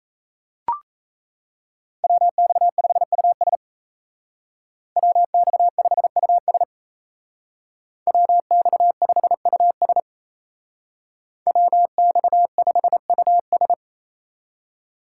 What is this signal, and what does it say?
Morse code, heard as a keyed tone, about 33 wpm, 700 Hz. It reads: WX5US WX5US WX5US WX5US
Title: Speed Racing - USA 2x2 Call Signs 28wpm